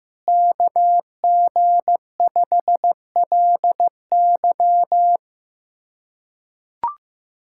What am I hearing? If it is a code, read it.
Morse code, heard as a keyed tone, about 15 wpm, 700 Hz. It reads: KG5LY
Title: USA Callsign DE Callsign - 5 Characters - 15wpm